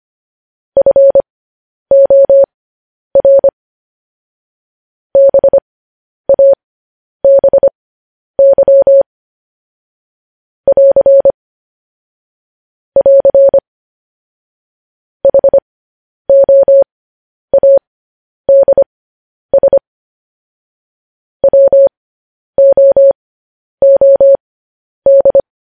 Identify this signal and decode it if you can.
Morse code, heard as a keyed tone, about 25 wpm, spaced as for 10 wpm, 550 Hz. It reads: FOR BABY <AR> <AR> HOADS WOOD
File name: podcast / News headlines-1x-25x10